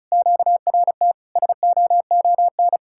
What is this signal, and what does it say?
Morse code, heard as a keyed tone, about 47 wpm, spaced as for 41 wpm, 700 Hz. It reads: QRT SOON
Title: Speed-Racing Common QSO Phrases - 35wpm